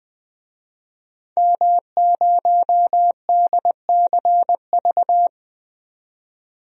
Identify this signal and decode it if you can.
Morse code, heard as a keyed tone, about 20 wpm, 700 Hz. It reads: M0DCV
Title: DXCC-100 Call Signs - 5 Characters - 20wpm